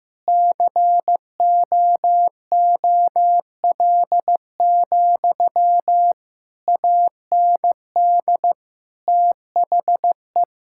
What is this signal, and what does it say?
Morse code, heard as a keyed tone, about 16 wpm, 700 Hz. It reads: COOL, AND THE
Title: Sentences from Top 1600 Words 15wpm